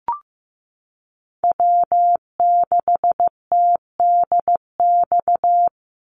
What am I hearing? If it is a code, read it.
Morse code, heard as a keyed tone, about 15 wpm, 700 Hz. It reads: W6TDX
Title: NA SOTA Chaser Callsigns 15wpm